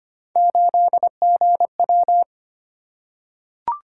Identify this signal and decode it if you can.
Morse code, heard as a keyed tone, about 25 wpm, 700 Hz. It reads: 8GW
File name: Three Letter-Number Combinations - 25wpm